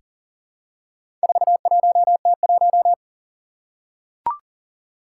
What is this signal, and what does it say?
Morse code, heard as a keyed tone, about 42 wpm, 700 Hz. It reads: 41T1
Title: Four Challenging Letter-Number Combinations 40wpm